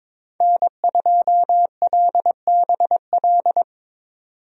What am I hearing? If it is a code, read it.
Morse code, heard as a keyed tone, about 22 wpm, 700 Hz. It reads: N2LBL
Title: USA Call Signs - 5 Characters - 22wpm